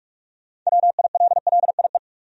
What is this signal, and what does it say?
Morse code, heard as a keyed tone, about 41 wpm, 700 Hz. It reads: WILLIE
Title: USA Common Names - 45wpm